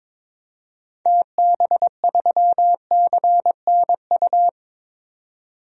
Morse code, heard as a keyed tone, about 22 wpm, 700 Hz. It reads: TB3CNU
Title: DXCC-100 Call Signs - 6 Characters - 22wpm